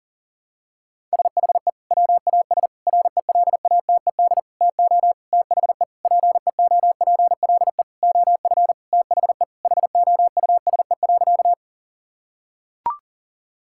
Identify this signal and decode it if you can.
Morse code, heard as a keyed tone, about 40 wpm, 700 Hz. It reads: SHE WAS RELATED TO THE PEOPLE OF THE HOUSE.
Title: Sentences from Top 1000 Words - Review Part 5 - 40wpm